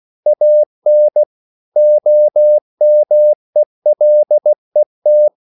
Morse code, heard as a keyed tone, about 16 wpm, 600 Hz. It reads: AN OMELET
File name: CW/Practice